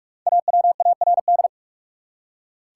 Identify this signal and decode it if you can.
Morse code, heard as a keyed tone, about 44 wpm, 700 Hz. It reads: AWARD
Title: Five Letter Words - 45wpm